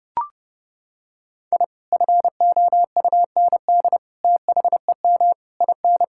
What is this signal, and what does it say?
Morse code, heard as a keyed tone, about 30 wpm, 700 Hz. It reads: I FOUND THEM IN
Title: Sentences from Top 1000 Words - Review Part 5 - 30wpm